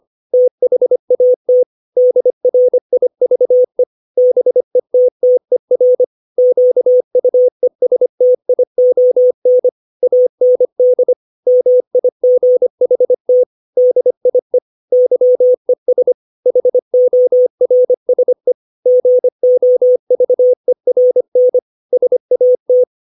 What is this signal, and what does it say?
Morse code, heard as a keyed tone, about 25 wpm, 500 Hz. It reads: THAT DRIVE BETTER QUESTION AND MIGHT DIE YES HORSE GOVERN SAT
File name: 632 words / 632 Words 25WPM